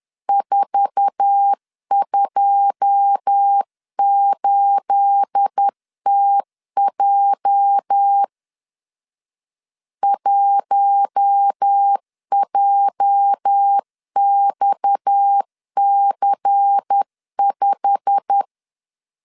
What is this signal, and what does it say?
Morse code, heard as a keyed tone, about 10 wpm, 800 Hz. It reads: 428TJ 1JXC5
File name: LEZIONI / 9 28 10wpm 2842